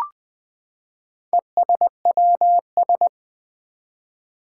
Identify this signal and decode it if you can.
Morse code, heard as a keyed tone, about 20 wpm, 700 Hz. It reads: ESWS
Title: Four Challenging Letter-Number Combinations 20wpm